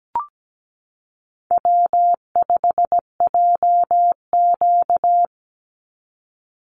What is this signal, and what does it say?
Morse code, heard as a keyed tone, about 17 wpm, 700 Hz. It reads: W5JQ
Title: Call Signs Repeated 3x - 17wpm